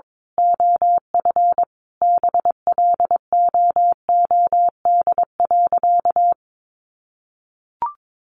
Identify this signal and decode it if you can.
Morse code, heard as a keyed tone, about 22 wpm, 700 Hz. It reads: OF BLOOD.
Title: A Sound Of Thunder 22wpm